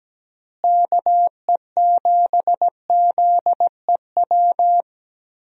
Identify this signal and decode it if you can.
Morse code, heard as a keyed tone, about 17 wpm, 700 Hz. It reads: KE7ZEW
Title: USA Call Signs - 6 Characters - Repeated 3x - 17wpm